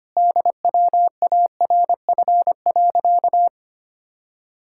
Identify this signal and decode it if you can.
Morse code, heard as a keyed tone, about 25 wpm, 700 Hz. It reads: DWARF.
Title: Pangram Challenge v2 - 25wpm